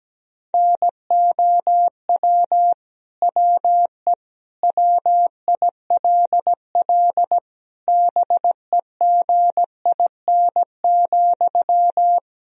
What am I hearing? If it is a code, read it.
Morse code, heard as a keyed tone, about 17 wpm, 700 Hz. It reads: NOW WE WILL BEGIN,
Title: Sentences from Top 1000 Words - Review Part 1 - 17wpm